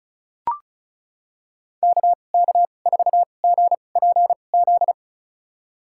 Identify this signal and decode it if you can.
Morse code, heard as a keyed tone, about 34 wpm, spaced as for 25 wpm, 700 Hz. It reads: KK4GPZ
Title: USA Call Signs - 6 Characters - 2x Character Spacing - 35wpm